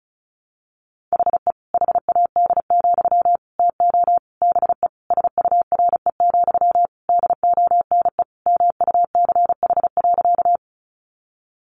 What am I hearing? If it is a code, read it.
Morse code, heard as a keyed tone, about 34 wpm, 700 Hz. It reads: HE HAD, TO BE SURE, DONE MUCH.
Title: Sentences from Top 500 Words 35wpm